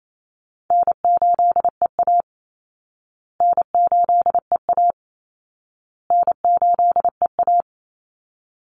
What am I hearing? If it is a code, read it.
Morse code, heard as a keyed tone, about 28 wpm, 700 Hz. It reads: N8EA N8EA N8EA
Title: Call Signs Repeated 2x - 28wpm